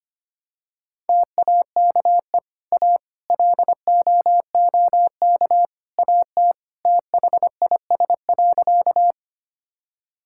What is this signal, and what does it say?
Morse code, heard as a keyed tone, about 25 wpm, 700 Hz. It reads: TAKE A LOOK AT THIS.
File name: practice / 25wpm / Sentences with 3-Word Phrases - 25wpm